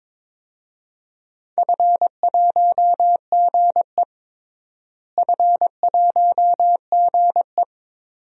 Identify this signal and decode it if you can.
Morse code, heard as a keyed tone, about 22 wpm, 700 Hz. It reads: F1GE F1GE
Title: DXCC-100 Call Signs - 4 Characters - Repeated 2x - 22wpm